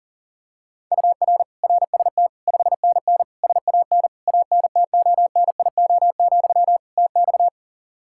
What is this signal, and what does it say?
Morse code, heard as a keyed tone, about 40 wpm, 700 Hz. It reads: UR RST 5NN SAN ANTONIO, TX
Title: Practice QSO - Exchange 1 - Beginning - 40wpm